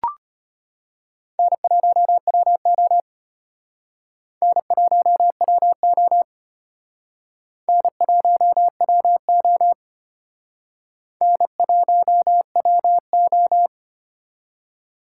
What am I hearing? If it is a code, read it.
Morse code, heard as a keyed tone, about 32 wpm, 700 Hz. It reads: N1WO N1WO N1WO N1WO